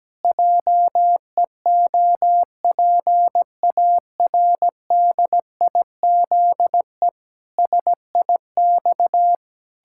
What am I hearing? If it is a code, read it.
Morse code, heard as a keyed tone, about 17 wpm, 700 Hz. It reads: JEOPARDIZE SIX
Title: Pangram Challenge v2 - 17wpm